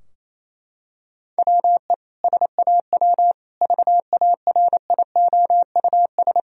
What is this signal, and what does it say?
Morse code, heard as a keyed tone, about 28 wpm, 700 Hz. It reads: WE SAW VARIOUS
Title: Sentences from Top 1000 Words - Review Part 4 - 28wpm